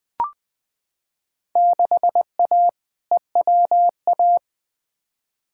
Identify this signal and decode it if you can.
Morse code, heard as a keyed tone, about 20 wpm, 700 Hz. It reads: 6A EWA
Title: Field Day Exchanges Encore - 20wpm